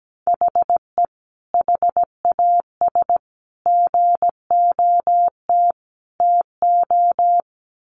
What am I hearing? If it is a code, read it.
Morse code, heard as a keyed tone, about 17 wpm, 700 Hz. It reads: HE HAS GOT TO
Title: Sentences from Top 800 Words 17wpm